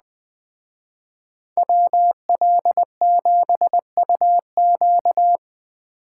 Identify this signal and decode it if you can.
Morse code, heard as a keyed tone, about 20 wpm, 700 Hz. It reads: WL7UQ